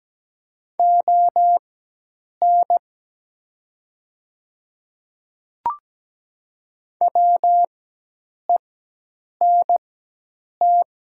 Morse code, heard as a keyed tone, about 17 wpm, spaced as for 8 wpm, 700 Hz. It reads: ON WENT
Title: 4x Character Spacing - Top 200 Words - 17wpm